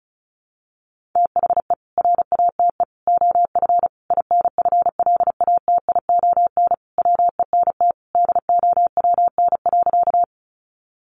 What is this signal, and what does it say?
Morse code, heard as a keyed tone, about 34 wpm, 700 Hz. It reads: THE RATE OF INFLATION WENT DOWN.
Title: Sentences from Top 1100 Words 35wpm